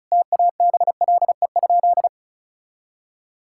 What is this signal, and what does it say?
Morse code, heard as a keyed tone, about 34 wpm, 700 Hz. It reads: TABLE?
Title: Sentences from Top 1600 Words 35wpm